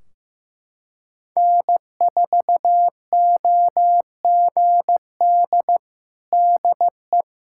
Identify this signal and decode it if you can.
Morse code, heard as a keyed tone, about 15 wpm, 700 Hz. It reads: N4OGD DE
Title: Practice QSO - Exchange 4 - Ending - 15wpm